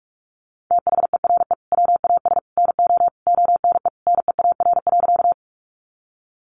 Morse code, heard as a keyed tone, about 44 wpm, 700 Hz. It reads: THERE WAS NO ONE NEAR.
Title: Sentences from Top 200 Words 45wpm